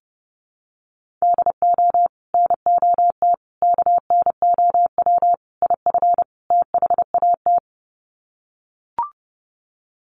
Morse code, heard as a keyed tone, about 30 wpm, 700 Hz. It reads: DO NOT KNOW IF THAT
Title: Sets of 5 Words Encore2 30wpm